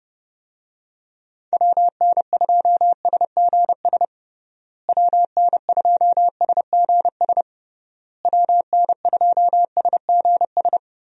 Morse code, heard as a keyed tone, about 30 wpm, 700 Hz. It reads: WN2SGS WN2SGS WN2SGS
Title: USA Call Signs - 6 Characters - Repeated 3x - 30wpm